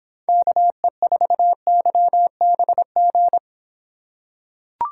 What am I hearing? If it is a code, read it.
Morse code, heard as a keyed tone, about 26 wpm, 700 Hz. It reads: KE4YBG